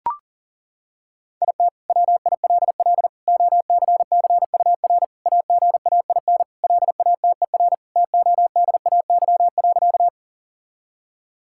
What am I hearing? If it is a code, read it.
Morse code, heard as a keyed tone, about 40 wpm, 700 Hz. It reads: IT WILL OCCUR AGAIN LATER TODAY.
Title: Sentences from Top 1000 Words - Review Part 6 - 40wpm